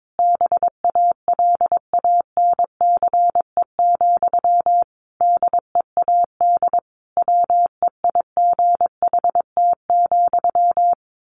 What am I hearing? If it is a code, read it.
Morse code, heard as a keyed tone, about 22 wpm, 700 Hz. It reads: BALANCE, DEAD WEIGHT,
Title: A Sound Of Thunder 22wpm